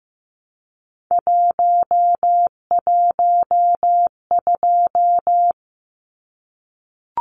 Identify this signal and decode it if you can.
Morse code, heard as a keyed tone, about 15 wpm, 700 Hz. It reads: 112
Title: Numbers - 3-Digit 15wpm